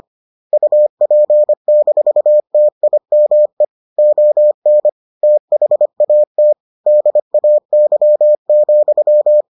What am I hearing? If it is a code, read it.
Morse code, heard as a keyed tone, about 25 wpm, 600 Hz. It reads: UP-TIME ON THAT DAY,